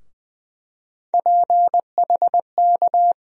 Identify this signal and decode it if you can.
Morse code, heard as a keyed tone, about 20 wpm, 700 Hz. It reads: PHK
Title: USA Call Sign Postfixes - 3 Letters 20wpm